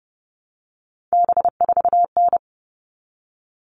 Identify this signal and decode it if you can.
Morse code, heard as a keyed tone, about 30 wpm, 700 Hz. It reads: B4N
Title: All Cut Numbers - 3-Digit 30wpm